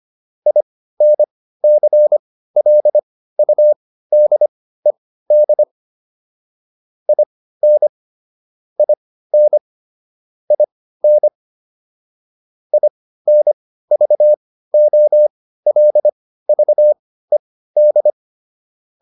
Morse code, heard as a keyed tone, about 25 wpm, spaced as for 15 wpm, 600 Hz. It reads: INCLUDED IN IN IN INVOLVED
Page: intermediate-practice-files